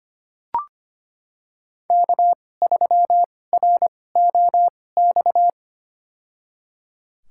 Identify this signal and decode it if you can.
Morse code, heard as a keyed tone, about 25 wpm, spaced as for 18 wpm, 700 Hz. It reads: K3ROX
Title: USA Call Signs - 5 Characters - 2x Character Spacing - 25wpm